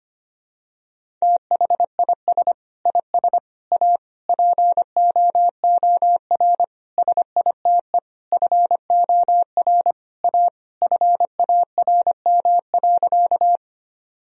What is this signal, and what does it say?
Morse code, heard as a keyed tone, about 25 wpm, 700 Hz. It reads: THIS IS A POOR SITE FOR A FARM.
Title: Sentences from Top 1000 Words - Review Part 4 - 25wpm